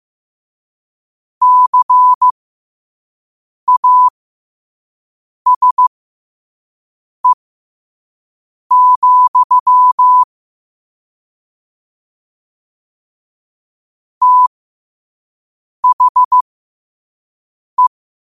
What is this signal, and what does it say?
Morse code, heard as a keyed tone, about 15 wpm, spaced as for 5 wpm, 1000 Hz. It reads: CASE, THE
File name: qotdfiles / 5wpm